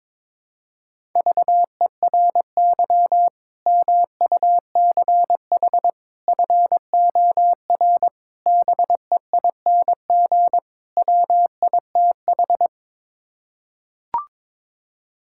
Morse code, heard as a keyed tone, about 22 wpm, 700 Hz. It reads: VERY MUCH FOR BEING WITH
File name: practice / 22wpm / Build A Phrase - 5-Word Phrases - 22wpm